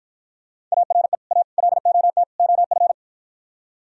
Morse code, heard as a keyed tone, about 52 wpm, 700 Hz. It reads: ARE A LOT OF